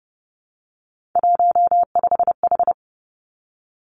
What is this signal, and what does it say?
Morse code, heard as a keyed tone, about 30 wpm, 700 Hz. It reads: 15H
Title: Three Challenging Letter-Number Combinations 30wpm